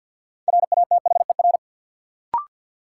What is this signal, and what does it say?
Morse code, heard as a keyed tone, about 53 wpm, 700 Hz. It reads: RATHER
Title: Sentences from Top 700 Words - New Word Drill - 50wpm